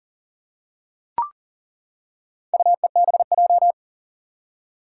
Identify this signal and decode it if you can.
Morse code, heard as a keyed tone, about 40 wpm, 700 Hz. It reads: UEBJ
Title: Four Challenging Letter-Number Combinations 40wpm